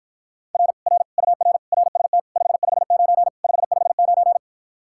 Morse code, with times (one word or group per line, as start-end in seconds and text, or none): 0.6-0.7 R
0.9-1.0 R
1.2-1.6 UR
1.7-2.2 RST
2.4-3.3 559
3.4-4.4 559